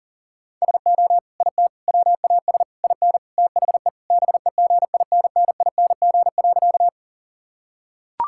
Morse code, heard as a keyed tone, about 40 wpm, 700 Hz. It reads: SO IT WAS IN THE BEGINNING.
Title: Sentences from Top 1000 Words - Review Part 5 - 40wpm